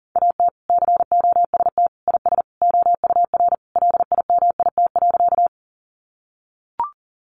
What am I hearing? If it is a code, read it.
Morse code, heard as a keyed tone, about 40 wpm, 700 Hz. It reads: AT COST IS OUR LIMIT.
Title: Sentences from Top 1500 Words 40wpm